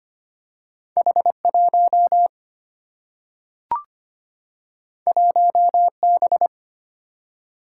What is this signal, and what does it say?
Morse code, heard as a keyed tone, about 25 wpm, 700 Hz. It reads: H1 1B
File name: Two Letter-Number Combinations - 25wpm